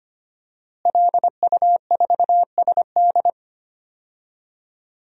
Morse code, heard as a keyed tone, about 25 wpm, 700 Hz. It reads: LU4SD
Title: DXCC-100 Call Signs - 5 Characters - Repeated 2x - 25wpm